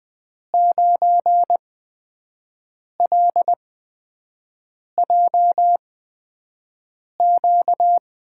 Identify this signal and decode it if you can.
Morse code, heard as a keyed tone, about 20 wpm, spaced as for 5 wpm, 700 Hz. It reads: 9LJQ
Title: USA Call Signs - 8x Character Spacing - 20wpm